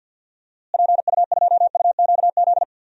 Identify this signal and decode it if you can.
Morse code, heard as a keyed tone, about 50 wpm, 700 Hz. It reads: PU2UQZ
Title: DXCC-100 Call Signs - 6 Characters - Repeated 3x - 50wpm